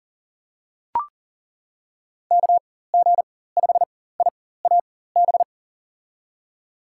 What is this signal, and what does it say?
Morse code, heard as a keyed tone, about 40 wpm, spaced as for 19 wpm, 700 Hz. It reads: KG5IAB
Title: USA Call Signs - 6 Characters - 4x Character Spacing - 40wpm